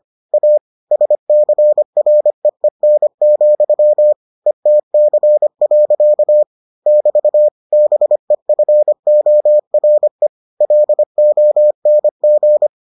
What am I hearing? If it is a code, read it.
Morse code, heard as a keyed tone, about 25 wpm, 600 Hz. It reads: A SCREEN, ETC. <BT> BEFORE LONG